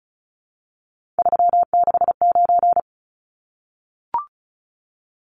Morse code, heard as a keyed tone, about 34 wpm, 700 Hz. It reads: 369
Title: Numbers - 3-Digit 35wpm